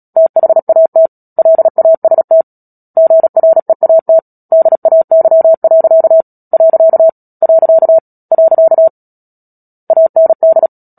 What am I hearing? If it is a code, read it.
Morse code, heard as a keyed tone, about 37 wpm, 650 Hz. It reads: THAT LAST GREAT DAY. . AND
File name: Book Two / 10-Chapter 10 - Epilogue - 36 WPM